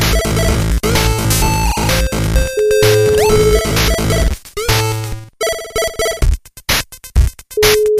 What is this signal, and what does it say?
Ringback tone (Europe, 425 Hz), with noise.